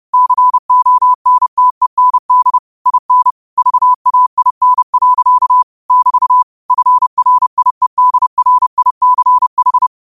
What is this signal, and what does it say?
Morse code, heard as a keyed tone, about 30 wpm, 1000 Hz. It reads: CONTEND IN VAIN. <BT> FRIEDRICH